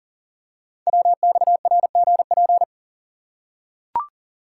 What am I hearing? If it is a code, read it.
Morse code, heard as a keyed tone, about 42 wpm, 690 Hz. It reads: WXRGP
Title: Code Groups - 5 Letters - 40wpm